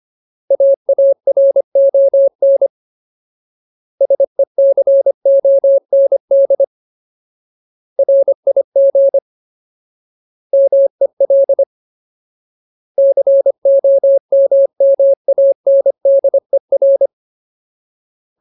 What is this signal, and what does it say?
Morse code, heard as a keyed tone, about 25 wpm, 550 Hz. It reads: AARON SECOND RIG MEL COMMANDER
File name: Ultimate Word List / 25WPM Ultimate